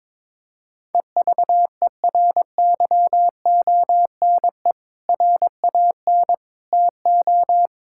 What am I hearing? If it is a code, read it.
Morse code, heard as a keyed tone, about 22 wpm, 700 Hz. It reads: EVERYONE RAN TO
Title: Sentences from Top 1600 Words 22wpm